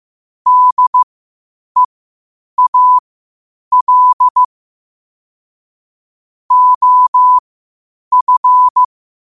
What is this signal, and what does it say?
Morse code, heard as a keyed tone, about 15 wpm, spaced as for 8 wpm, 1000 Hz. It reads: DEAL OF
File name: qotdfiles / 8wpm